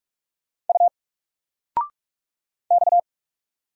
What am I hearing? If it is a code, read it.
Morse code, heard as a keyed tone, about 44 wpm, 700 Hz. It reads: U X
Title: Single Letter-Number 45wpm